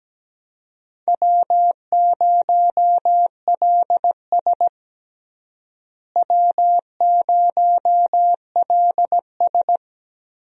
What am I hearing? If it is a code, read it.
Morse code, heard as a keyed tone, about 17 wpm, 700 Hz. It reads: W0LS W0LS
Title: USA Call Signs - 4 Characters - Repeated 2x - 17wpm